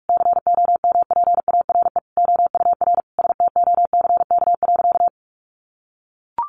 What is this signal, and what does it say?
Morse code, heard as a keyed tone, about 44 wpm, 700 Hz. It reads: COMPARE OUR STOCK.
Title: Sentences from Top 1300 Words 45wpm